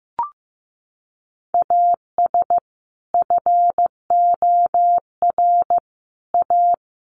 Morse code, heard as a keyed tone, about 15 wpm, 700 Hz. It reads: AS FOR A
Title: Sentences from Top 1400 Words 15wpm